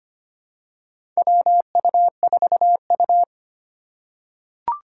Morse code, heard as a keyed tone, about 25 wpm, 700 Hz. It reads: WU4U
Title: USA Call Signs - 3x Character Spacing - 25wpm